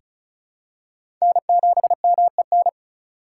Morse code, heard as a keyed tone, about 36 wpm, 700 Hz. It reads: N7MEN
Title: USA Call Signs - 5 Characters - 4x Character Spacing - 35wpm